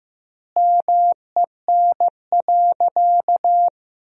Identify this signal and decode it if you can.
Morse code, heard as a keyed tone, about 15 wpm, 700 Hz. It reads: MEN.
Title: 4x Word-Spacing - Sentences from Top 300 Words - 15wpm